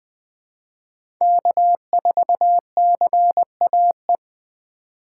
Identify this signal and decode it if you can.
Morse code, heard as a keyed tone, about 20 wpm, 700 Hz. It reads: K4CAE
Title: 2023 POTA Hunter Call Signs - 20wpm